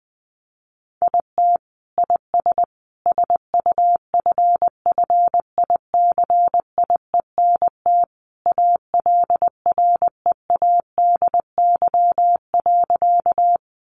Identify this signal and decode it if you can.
Morse code, heard as a keyed tone, about 19 wpm, 700 Hz. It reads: IT IS SUFFICIENT ALREADY.
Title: Sentences from Top 1400 Words 20wpm